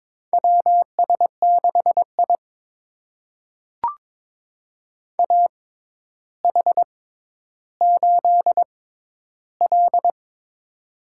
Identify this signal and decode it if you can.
Morse code, heard as a keyed tone, about 22 wpm, spaced as for 11 wpm, 700 Hz. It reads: WS6I AH8L
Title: USA Call Signs - 4 Characters - 6x Character Spacing - 22wpm